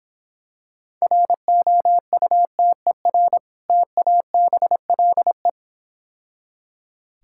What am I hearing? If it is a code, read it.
Morse code, heard as a keyed tone, about 26 wpm, 700 Hz. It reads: ROUTER TABLE